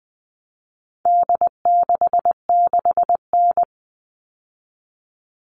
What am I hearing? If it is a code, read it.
Morse code, heard as a keyed tone, about 20 wpm, 700 Hz. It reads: D66N
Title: All Cut Numbers - 4-Digit 20wpm